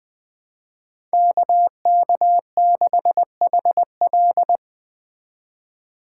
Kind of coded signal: Morse code

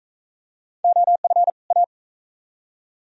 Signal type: Morse code

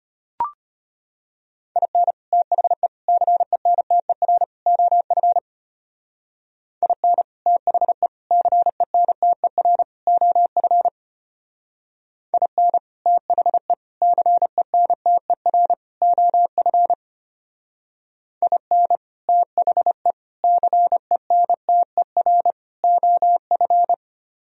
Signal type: Morse code

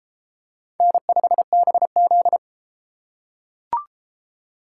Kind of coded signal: Morse code

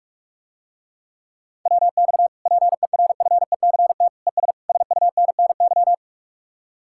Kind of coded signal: Morse code

